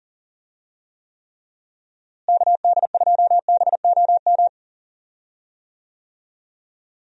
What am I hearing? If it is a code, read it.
Morse code, heard as a keyed tone, about 40 wpm, 700 Hz. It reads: KD2BOM
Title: USA Call Signs - 6 Characters - Repeated 3x - 40wpm